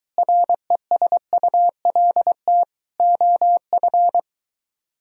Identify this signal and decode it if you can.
Morse code, heard as a keyed tone, about 22 wpm, 700 Hz. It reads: RESULT OF